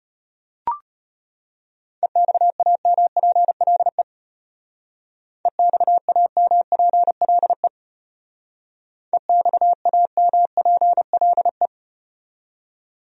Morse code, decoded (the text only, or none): EXAMPLE EXAMPLE EXAMPLE